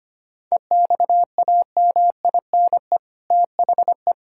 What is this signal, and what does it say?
Morse code, heard as a keyed tone, about 25 wpm, 700 Hz. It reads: EXAMINE THE
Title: Sentences from Top 1600 Words 25wpm